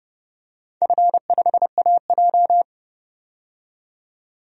Morse code, heard as a keyed tone, about 30 wpm, 700 Hz. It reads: F5AJ
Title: DXCC-100 Call Signs - 4 Characters - 30wpm